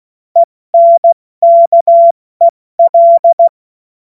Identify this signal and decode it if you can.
Morse code, heard as a keyed tone, about 16 wpm, spaced as for 14 wpm, 680 Hz. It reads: ENKEL
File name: Morsekurs / Witze / Witz 19. Oktober 2021, 70 F80